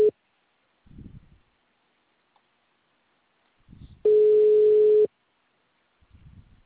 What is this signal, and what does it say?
Ringback tone (Europe, 425 Hz).